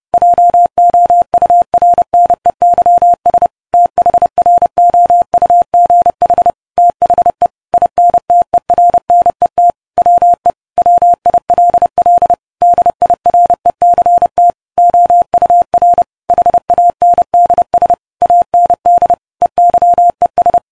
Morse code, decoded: JOURNEYS THROUGH THE INTERNET WE WILL DIRECT OUR HANDS AND EYES